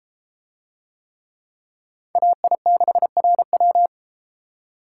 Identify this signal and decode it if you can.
Morse code, heard as a keyed tone, about 34 wpm, 700 Hz. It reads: AI6RW